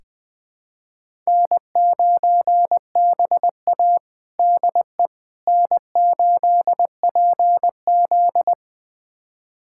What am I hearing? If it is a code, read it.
Morse code, heard as a keyed tone, about 20 wpm, 700 Hz. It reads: N9BA DE N8PZ
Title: USA Callsign DE Callsign - 4 Characters - 20wpm